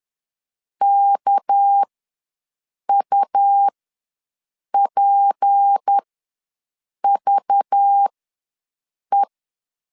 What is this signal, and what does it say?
Morse code, heard as a keyed tone, about 10 wpm, spaced as for 6 wpm, 800 Hz. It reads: KUPVE
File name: LEZIONI / 9 28 ws 10wpm 4124